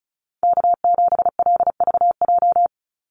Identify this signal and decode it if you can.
Morse code, heard as a keyed tone, about 34 wpm, 700 Hz. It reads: K7LVJ
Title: Call Signs Repeated 3x - 35wpm